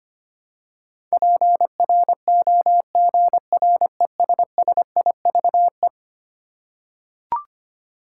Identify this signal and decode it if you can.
Morse code, heard as a keyed tone, about 25 wpm, 700 Hz. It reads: PROGRESSIVE
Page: morse-code-ninja-practice-25wpm